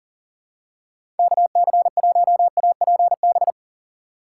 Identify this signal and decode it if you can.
Morse code, heard as a keyed tone, about 40 wpm, 700 Hz. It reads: KC1APB
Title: USA Call Signs - 3x Character Spacing - 40wpm